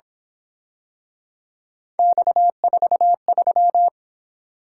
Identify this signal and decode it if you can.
Morse code, heard as a keyed tone, about 27 wpm, 700 Hz. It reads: X43
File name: Speed-Racing - Three Letter-Number Combinations - 22wpm